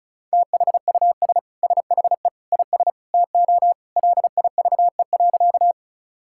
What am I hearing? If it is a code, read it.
Morse code, heard as a keyed tone, about 36 wpm, 700 Hz. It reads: THUS SHE IS TO LIVE.
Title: Sentences from Top 1000 Words - Review Part 5 - 35wpm